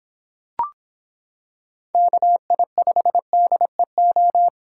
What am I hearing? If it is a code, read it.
Morse code, heard as a keyed tone, about 26 wpm, 700 Hz. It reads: KI5DEO